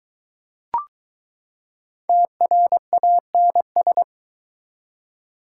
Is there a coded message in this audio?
Yes: Morse code from 2.1 to 4.0 s, heard as a keyed tone, about 21 wpm, 700 Hz. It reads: TRANS